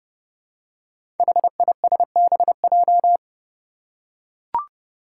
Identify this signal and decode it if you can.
Morse code, heard as a keyed tone, about 32 wpm, 700 Hz. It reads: HISBJ
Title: Five Challenging Letter-Number Combinations 30wpm